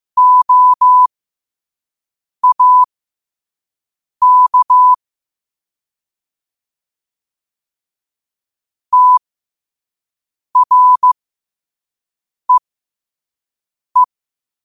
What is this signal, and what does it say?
Morse code, heard as a keyed tone, about 15 wpm, spaced as for 5 wpm, 1000 Hz. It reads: OAK TREE
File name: qotdfiles / 5wpm